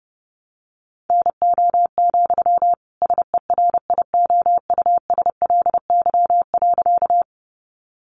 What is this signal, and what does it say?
Morse code, heard as a keyed tone, about 30 wpm, 700 Hz. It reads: NO, SERIOUSLY.
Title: Sentences from Top 1200 Words 30wpm